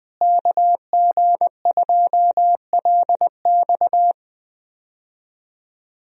Morse code, heard as a keyed tone, about 21 wpm, 700 Hz. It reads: KG2LX
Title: USA Call Signs - 5 Characters - Repeated 3x - 20wpm